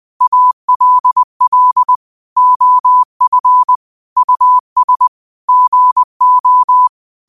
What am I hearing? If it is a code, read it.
Morse code, heard as a keyed tone, about 20 wpm, 1000 Hz. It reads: ALL OF US GO
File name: qotdfiles / 20wpm